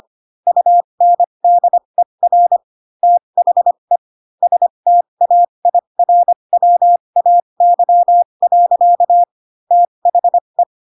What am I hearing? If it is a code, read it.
Morse code, heard as a keyed tone, about 25 wpm, spaced as for 22 wpm, 700 Hz. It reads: UNDER THE STAIRWAY. THE
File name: Pinocchio / Pinocchio ch3 18 25 ik1hge